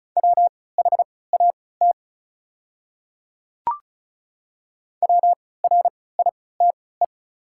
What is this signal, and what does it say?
Morse code, heard as a keyed tone, about 36 wpm, spaced as for 20 wpm, 690 Hz. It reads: WHAT WRITE
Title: 3x Character Spacing - Top 100 Words - 35wpm